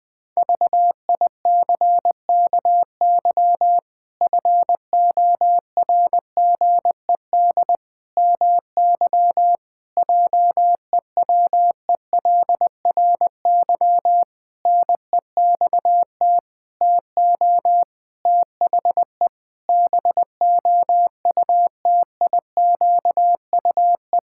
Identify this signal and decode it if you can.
Morse code, heard as a keyed tone, about 20 wpm, 700 Hz. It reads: VICKY FORGED MY JEWELRY NEXT TO THE BOUTIQUE
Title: Pangram Challenge v2 - 20wpm